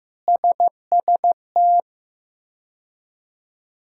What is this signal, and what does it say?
Morse code, heard as a keyed tone, about 16 wpm, 700 Hz. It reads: SST
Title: K1USN SST QSO Practice - 15wpm